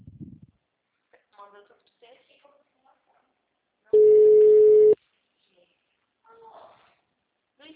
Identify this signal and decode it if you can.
Ringback tone (Europe, 425 Hz), with speech.
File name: out- -2006- - - .1913